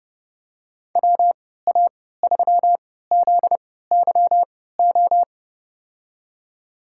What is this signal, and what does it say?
Morse code, heard as a keyed tone, about 30 wpm, spaced as for 17 wpm, 700 Hz. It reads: WA3ZYO
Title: USA Call Signs - 6 Characters - 3x Character Spacing - 30wpm